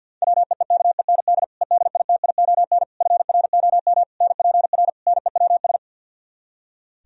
Morse code, heard as a keyed tone, about 64 wpm, 700 Hz. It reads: WEEKEND EDITION FROM NPR NEWS